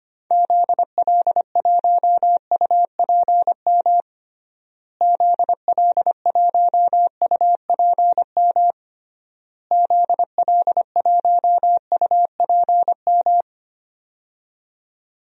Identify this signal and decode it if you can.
Morse code, heard as a keyed tone, about 25 wpm, 700 Hz. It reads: ZL1UPM ZL1UPM ZL1UPM
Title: DXCC-100 Call Signs - 6 Characters - Repeated 3x - 25wpm